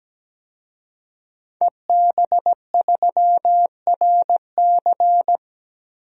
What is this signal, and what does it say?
Morse code, heard as a keyed tone, about 17 wpm, 700 Hz. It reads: EB3RC
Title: DXCC-100 Call Signs - 5 Characters - 17wpm